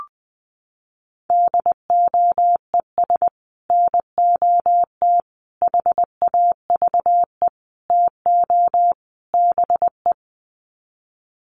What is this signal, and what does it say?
Morse code, heard as a keyed tone, about 20 wpm, 700 Hz. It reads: DOES NOT HAVE TO BE
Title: Sets of 5 Words 20wpm